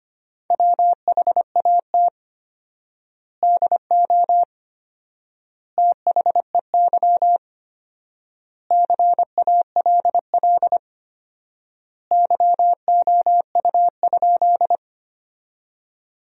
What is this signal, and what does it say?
Morse code, heard as a keyed tone, about 25 wpm, 700 Hz. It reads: WHAT DO THEY CALL YOU?
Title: 4x Word-Spacing - Sentences from Top 100 Words - 25wpm